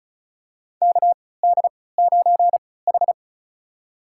Morse code, heard as a keyed tone, about 35 wpm, spaced as for 20 wpm, 700 Hz. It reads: KD9H
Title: USA Call Signs - 3x Character Spacing - 35wpm